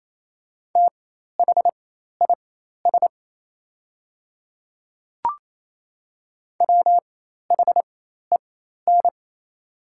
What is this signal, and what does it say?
Morse code, heard as a keyed tone, about 28 wpm, spaced as for 13 wpm, 700 Hz. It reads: THIS WHEN
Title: 4x Character Spacing - Top 100 Words - 28wpm